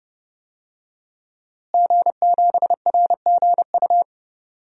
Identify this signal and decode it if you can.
Morse code, heard as a keyed tone, about 30 wpm, 700 Hz. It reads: G7RGU
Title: DXCC-100 Call Signs - 5 Characters - Repeated 3x - 30wpm